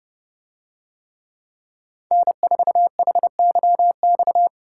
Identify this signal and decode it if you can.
Morse code, heard as a keyed tone, about 29 wpm, 700 Hz. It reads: N4HYX